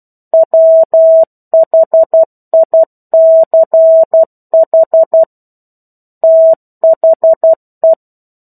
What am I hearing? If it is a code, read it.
Morse code, heard as a keyed tone, about 12 wpm, 650 Hz. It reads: WHICH THE
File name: 01-Chapter-01 12 WPM